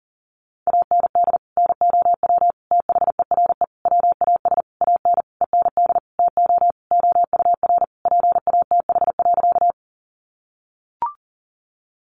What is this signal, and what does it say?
Morse code, heard as a keyed tone, about 40 wpm, 700 Hz. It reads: AND NOW THERE WAS AN END TO OUR PATH.
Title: Sentences from Top 1400 Words 40wpm